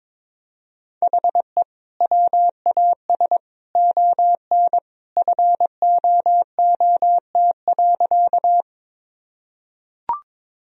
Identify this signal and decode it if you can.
Morse code, heard as a keyed tone, about 21 wpm, 700 Hz. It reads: HE WAS ON FOOT.
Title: Sentences from Top 1000 Words - Review Part 3 - 22wpm